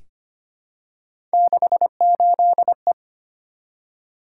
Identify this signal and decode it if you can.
Morse code, heard as a keyed tone, about 25 wpm, 700 Hz. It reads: 68E
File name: Three Letter-Number Combinations - 25wpm